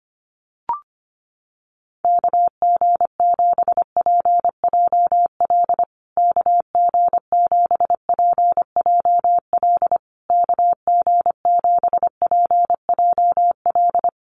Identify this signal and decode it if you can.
Morse code, heard as a keyed tone, about 25 wpm, 700 Hz. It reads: KG7PJL KG7PJL KG7PJL
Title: USA Call Signs Repeated 3x 25wpm